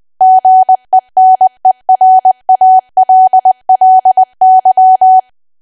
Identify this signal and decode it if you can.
Morse code, heard as a keyed tone, about 20 wpm, 750 Hz. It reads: GENERALLY
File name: cwa / W1AW-Level3 / 20WPM